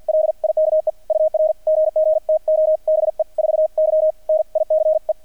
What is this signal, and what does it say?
Morse code, heard as a keyed tone, about 48 wpm, 620 Hz. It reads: LIKE UR GG TO BE 4Y NICE
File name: archived / CRSnet